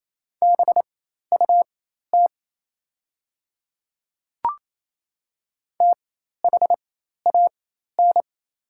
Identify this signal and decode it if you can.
Morse code, heard as a keyed tone, about 28 wpm, spaced as for 13 wpm, 700 Hz. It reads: BUT THAN
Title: 4x Character Spacing - Top 100 Words - 28wpm